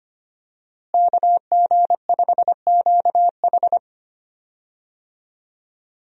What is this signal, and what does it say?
Morse code, heard as a keyed tone, about 25 wpm, 700 Hz. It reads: KG5QH
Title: USA Call Signs - 5 Characters - Repeated 3x - 25wpm